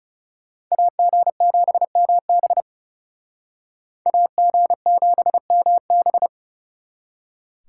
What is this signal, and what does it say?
Morse code, heard as a keyed tone, about 38 wpm, 700 Hz. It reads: AG7MB AG7MB